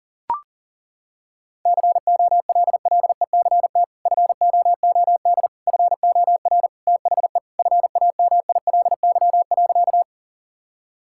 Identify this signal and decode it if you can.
Morse code, heard as a keyed tone, about 37 wpm, 700 Hz. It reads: COLLECT FOOD FOR THE FAMILY.
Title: Sentences from Top 1600 Words 40wpm